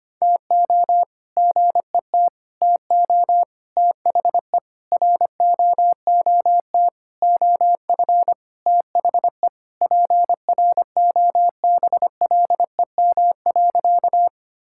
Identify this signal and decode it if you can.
Morse code, heard as a keyed tone, about 25 wpm, 700 Hz. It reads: TO GET TO THE ROOT OF THE PROBLEM.
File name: practice / 25wpm / All-Age Appropriate Jokes - 25wpm